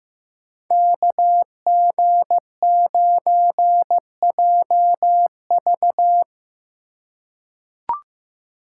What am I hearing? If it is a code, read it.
Morse code, heard as a keyed tone, about 15 wpm, 700 Hz. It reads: KG9JV
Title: USA Call Signs - 5 Characters - 4x Character Spacing - 15wpm